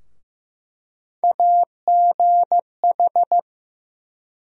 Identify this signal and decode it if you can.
Morse code, heard as a keyed tone, about 16 wpm, 700 Hz. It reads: AGH
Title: USA Call Sign Postfixes - 3 Letters 15wpm